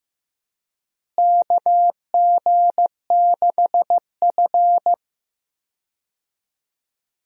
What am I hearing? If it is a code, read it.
Morse code, heard as a keyed tone, about 15 wpm, 700 Hz. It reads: KG6F
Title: USA Call Signs - 4 Characters - Repeated 2x - 15wpm